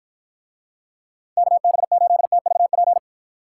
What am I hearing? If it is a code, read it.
Morse code, heard as a keyed tone, about 53 wpm, 700 Hz. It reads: KB8TVP